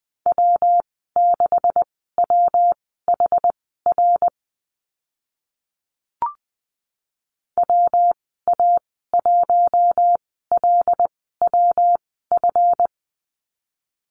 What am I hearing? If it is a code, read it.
Morse code, heard as a keyed tone, about 20 wpm, spaced as for 15 wpm, 700 Hz. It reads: W6WHR WA1LWF
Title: 140 Lesson 23g - Letter f - Call Signs 20wpm